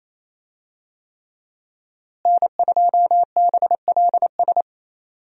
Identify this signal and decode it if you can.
Morse code, heard as a keyed tone, about 27 wpm, 700 Hz. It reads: N2BLS
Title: USA Call Signs - 5 Characters - Repeated 3x - 28wpm